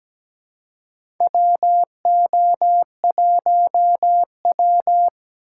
Morse code, heard as a keyed tone, about 17 wpm, 700 Hz. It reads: WO1W
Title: USA Call Signs - 4 Characters - 2x Character Spacing - 17wpm